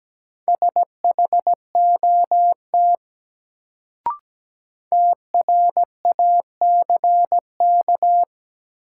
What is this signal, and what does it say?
Morse code, heard as a keyed tone, about 17 wpm, 700 Hz. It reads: SHOT TRACK
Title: Sentences from Top 1100 Words - New Word Drill - 17wpm